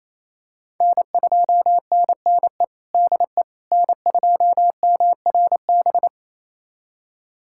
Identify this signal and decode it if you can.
Morse code, heard as a keyed tone, about 28 wpm, 700 Hz. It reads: N2NNE DE N2MRB
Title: USA Callsign DE Callsign - 5 Characters - 28wpm